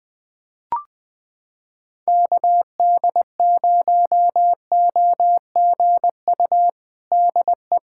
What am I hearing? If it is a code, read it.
Morse code, heard as a keyed tone, about 21 wpm, 700 Hz. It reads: KD0OGU DE
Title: USA Callsign DE Callsign - 6 Characters - 20wpm